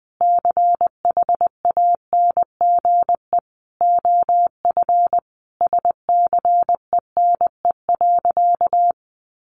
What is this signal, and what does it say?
Morse code, heard as a keyed tone, about 20 wpm, 700 Hz. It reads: CHANGE OF SCENE.
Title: Sentences from Top 1100 Words 20wpm